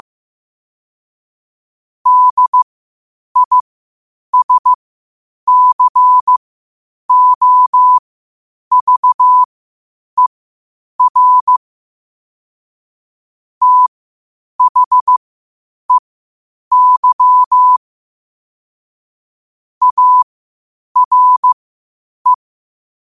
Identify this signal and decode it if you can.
Morse code, heard as a keyed tone, about 15 wpm, spaced as for 8 wpm, 1000 Hz. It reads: DISCOVER THEY ARE
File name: qotdfiles / 8wpm